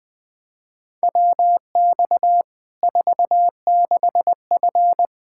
Morse code, heard as a keyed tone, about 20 wpm, 700 Hz. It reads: WX 46F
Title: Practice QSO - Exchange 2 - Info - 20wpm